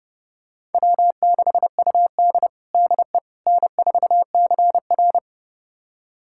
Morse code, heard as a keyed tone, about 30 wpm, 700 Hz. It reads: W6UD DE N4CR
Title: USA Callsign DE Callsign - 4 Characters - 30wpm